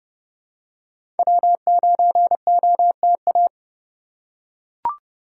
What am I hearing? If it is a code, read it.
Morse code, heard as a keyed tone, about 30 wpm, 700 Hz. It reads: W9OTA
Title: USA Call Signs - 5 Characters - 2x Character Spacing - 30wpm